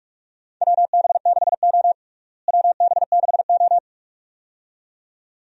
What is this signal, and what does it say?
Morse code, heard as a keyed tone, about 46 wpm, 700 Hz. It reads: WB6O WB6O
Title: USA Call Signs - 4 Characters - Repeated 2x - 45wpm